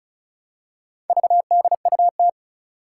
Morse code, heard as a keyed tone, about 36 wpm, 690 Hz. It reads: VDUT
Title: Four Challenging Letter-Number Combinations 35wpm